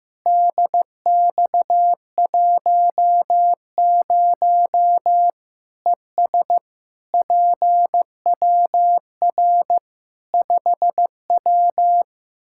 Morse code, heard as a keyed tone, about 15 wpm, 700 Hz. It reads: DX10 ES PWR 5W
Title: Practice QSO - Exchange 2 - Info - 15wpm